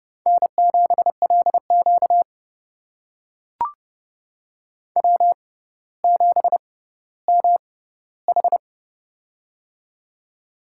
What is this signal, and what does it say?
Morse code, heard as a keyed tone, about 30 wpm, spaced as for 15 wpm, 700 Hz. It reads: N7LQ W7MH
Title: USA Call Signs - 4 Characters - 6x Character Spacing - 30wpm